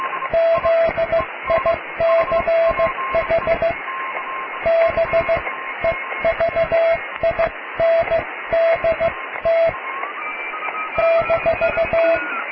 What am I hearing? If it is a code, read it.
Morse code, heard as a keyed tone, about 15 wpm, spaced as for 13 wpm, 660 Hz, with radio noise. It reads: ZICH BEVINDT -
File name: cw / tekst(a)+QRM